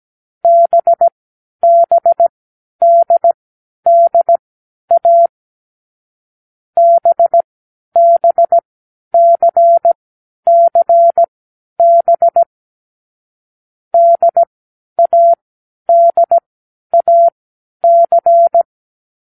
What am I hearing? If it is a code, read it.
Morse code, heard as a keyed tone, about 17 wpm, spaced as for 10 wpm, 680 Hz. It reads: BBDDA BBCCB DADAC
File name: k7qo / Lesson6(A-D)